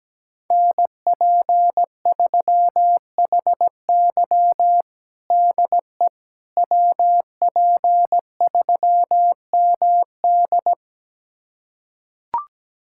Morse code, heard as a keyed tone, about 17 wpm, 700 Hz. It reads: NP3HY DE WP3MD
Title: USA Callsign DE Callsign - 5 Characters - 17wpm